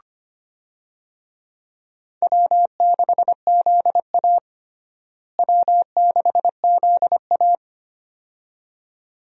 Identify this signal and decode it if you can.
Morse code, heard as a keyed tone, about 25 wpm, 700 Hz. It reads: W6ZA W6ZA
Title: USA Call Signs - 4 Characters - Repeated 2x - 25wpm